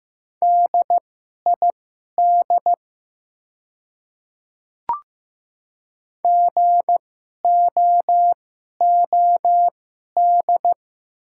Morse code, heard as a keyed tone, about 15 wpm, spaced as for 11 wpm, 700 Hz. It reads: DID GOOD
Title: 2x Character Spacing - Top 300 Words - 15wpm